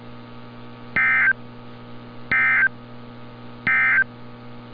Emergency Alert System (SAME) tones, heard as harsh buzzing bursts. It is heard from 0.8 to 4.1 s.
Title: NOAA Weather Radio - Altoona, PA